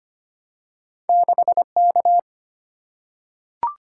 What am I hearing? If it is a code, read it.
Morse code, heard as a keyed tone, about 25 wpm, 700 Hz. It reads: 6K
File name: Two Letter-Number Combinations - 25wpm